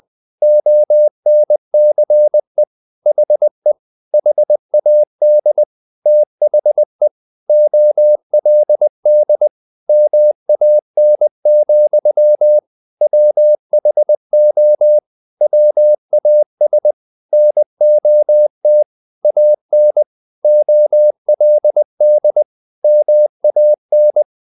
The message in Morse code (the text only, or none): ONCE HE HAD THE OLD MAN, WHO WAS NOT AN OLD MAN